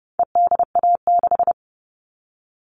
Morse code, heard as a keyed tone, about 30 wpm, 700 Hz. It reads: EDA6
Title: All Cut Numbers - 4-Digit 30wpm